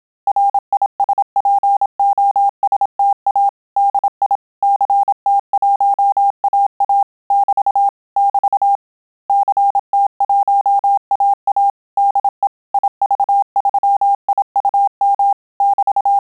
Morse code, heard as a keyed tone, about 27 wpm, 800 Hz. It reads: RISPOSTA DI CT1AA <BT> <BT> CT1AA DE IV3IUM <BT>
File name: Telegraphy-group / QSO CW 25 PPM